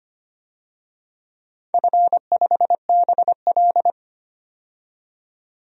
Morse code, heard as a keyed tone, about 25 wpm, 700 Hz. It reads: F5BL
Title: DXCC-100 Call Signs - 4 Characters - 25wpm